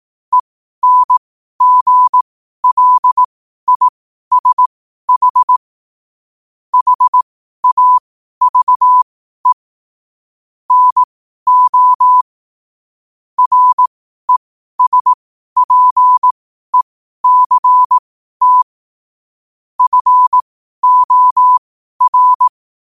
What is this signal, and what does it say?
Morse code, heard as a keyed tone, about 18 wpm, spaced as for 12 wpm, 1000 Hz. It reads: ENGLISH HAVE NO RESPECT FOR
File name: qotdfiles / 12wpm